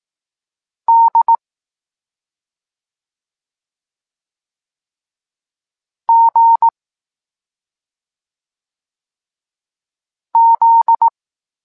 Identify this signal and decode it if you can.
Morse code, heard as a keyed tone, about 18 wpm, 920 Hz. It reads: D G Z
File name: Tekst-26 1